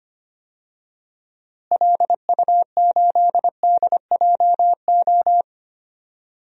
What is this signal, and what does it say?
Morse code, heard as a keyed tone, about 25 wpm, 700 Hz. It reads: LU8DJO
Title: DXCC-100 Call Signs - 6 Characters - Repeated 3x - 25wpm